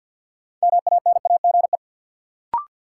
Morse code, heard as a keyed tone, about 47 wpm, 700 Hz. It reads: MANAGE
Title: Sentences from Top 1400 Words - New Word Drill - 50wpm